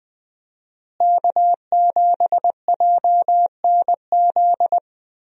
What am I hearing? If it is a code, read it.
Morse code, heard as a keyed tone, about 21 wpm, 700 Hz. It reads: K7JNZ
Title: USA Call Signs - 5 Characters - Repeated 3x - 20wpm